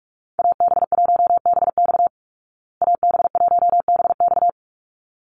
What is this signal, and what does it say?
Morse code, heard as a keyed tone, about 44 wpm, 700 Hz. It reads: AB1BX AB1BX
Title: Call Signs Repeated 3x - 45wpm